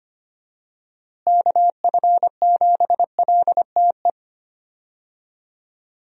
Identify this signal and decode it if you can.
Morse code, heard as a keyed tone, about 25 wpm, 700 Hz. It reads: KF7LTE
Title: USA Call Signs - 6 Characters - Repeated 2x - 25wpm